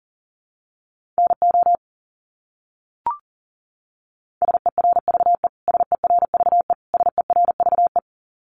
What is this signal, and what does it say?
Morse code, heard as a keyed tone, about 40 wpm, 700 Hz. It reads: NO SERVE SERVE SERVE
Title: Top 500 Words Repeated 40wpm